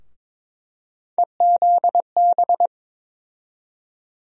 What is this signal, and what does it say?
Morse code, heard as a keyed tone, about 22 wpm, 700 Hz. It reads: EZB
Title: USA Call Sign Postfixes - 3 Letters 22wpm